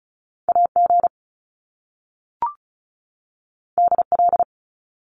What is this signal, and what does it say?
Morse code, heard as a keyed tone, about 35 wpm, 700 Hz. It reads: AG DL
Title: Top Two Letter Combinations in Words 35wpm